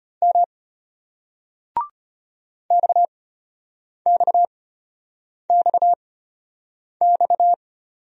Morse code, heard as a keyed tone, about 33 wpm, spaced as for 8 wpm, 700 Hz. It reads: M XXXX